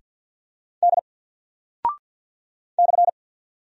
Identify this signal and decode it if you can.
Morse code, heard as a keyed tone, about 48 wpm, 700 Hz. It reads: D /
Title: Single Character (Letter:Number:Punctuation:Prosign) 50wpm